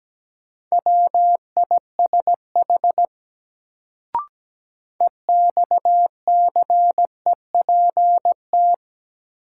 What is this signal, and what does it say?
Morse code, heard as a keyed tone, about 17 wpm, 700 Hz. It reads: WISH EXCEPT
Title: Sentences from Top 1600 Words - New Word Drill - 17wpm